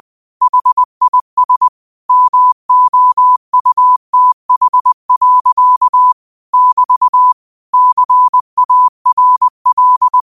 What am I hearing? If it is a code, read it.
Morse code, heard as a keyed tone, about 20 wpm, 1000 Hz. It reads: HIS MOUTH. <BT> CARL